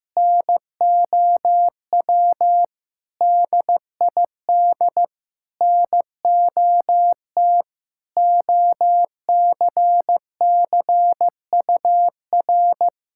Morse code, heard as a keyed tone, about 15 wpm, 700 Hz. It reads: NOW DID NOT OCCUR
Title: Sentences from Top 1000 Words - Review Part 6 - 15wpm